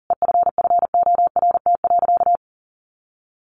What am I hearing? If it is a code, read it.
Morse code, heard as a keyed tone, about 40 wpm, 700 Hz. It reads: EFFORT.
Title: Sentences from Top 1200 Words 40wpm